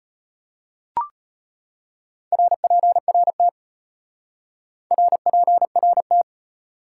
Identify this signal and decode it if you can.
Morse code, heard as a keyed tone, about 32 wpm, 700 Hz. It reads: RPRT RPRT